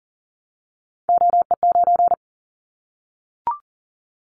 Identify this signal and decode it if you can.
Morse code, heard as a keyed tone, about 39 wpm, 700 Hz. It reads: OE9
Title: Callsign Prefixes for Top 25 DXCC - Repeated 3x - 40wpm